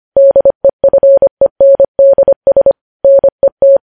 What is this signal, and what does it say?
Morse code, heard as a keyed tone, about 25 wpm, 550 Hz. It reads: DEFENDS NET
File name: podcast / News headlines-1x-25x25